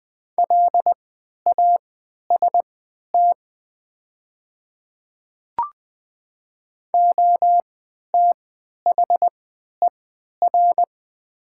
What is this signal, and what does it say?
Morse code, heard as a keyed tone, about 20 wpm, spaced as for 11 wpm, 700 Hz. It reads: LAST OTHER
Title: 3x Character Spacing - Top 300 Words - 20wpm